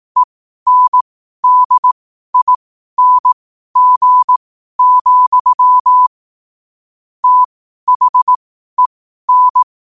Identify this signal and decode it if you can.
Morse code, heard as a keyed tone, about 18 wpm, spaced as for 12 wpm, 1000 Hz. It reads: ENDING, THEN
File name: qotdfiles / 12wpm